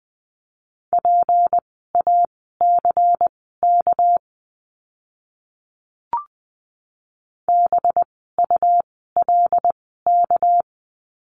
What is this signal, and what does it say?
Morse code, heard as a keyed tone, about 20 wpm, spaced as for 14 wpm, 700 Hz. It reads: PACK BULK
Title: 199 Lesson 32f - Letter k - Words 20wpm